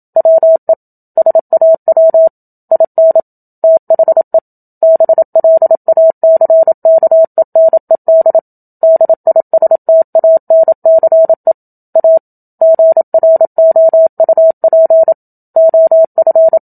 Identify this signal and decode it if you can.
Morse code, heard as a keyed tone, about 27 wpm, 650 Hz. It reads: WE SAW IN THE BLACKENED DISTANCE A GROUP OF
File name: Book Two / 01-Chapter 1 - 27 WPM